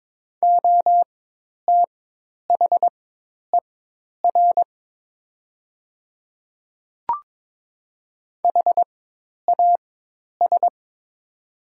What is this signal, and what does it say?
Morse code, heard as a keyed tone, about 22 wpm, spaced as for 10 wpm, 700 Hz. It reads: OTHER HAS
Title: 4x Character Spacing - Top 100 Words - 22wpm